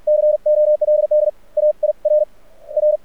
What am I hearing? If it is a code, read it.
Morse code, heard as a keyed tone, about 48 wpm, spaced as for 32 wpm, 600 Hz. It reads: COLD AIRR
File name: archived / CRSnet